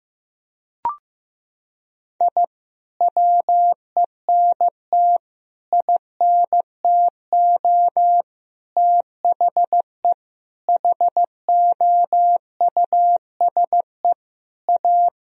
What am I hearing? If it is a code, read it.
Morse code, heard as a keyed tone, about 16 wpm, 700 Hz. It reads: I WENT INTO THE HOUSE A
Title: Sentences from Top 1000 Words - Review Part 1 - 15wpm